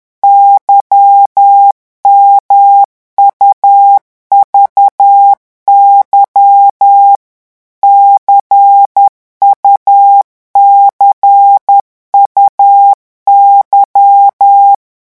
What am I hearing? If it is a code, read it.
Morse code, heard as a keyed tone, about 10 wpm, 800 Hz. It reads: YMUVY CUCUY
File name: morse mp3 / revision2-vitesse 10 mots